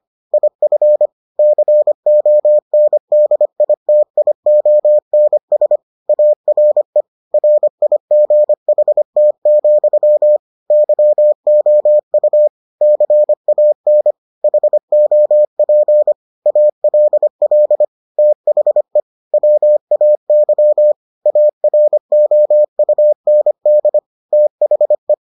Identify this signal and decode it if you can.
Morse code, heard as a keyed tone, about 25 wpm, 600 Hz. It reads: IF CONDITIONS ARE RIGHT, YOU CAN HOP ALL THE WAY AROUND THE